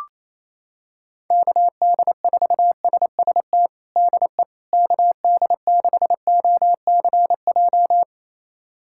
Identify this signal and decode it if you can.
Morse code, heard as a keyed tone, about 28 wpm, 700 Hz. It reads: KD4SST DE KD6OCJ
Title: USA Callsign DE Callsign - 6 Characters - 28wpm